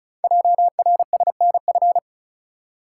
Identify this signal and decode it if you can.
Morse code, heard as a keyed tone, about 36 wpm, 700 Hz. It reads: JRSNF
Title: Code Groups - 5 Letters - 35wpm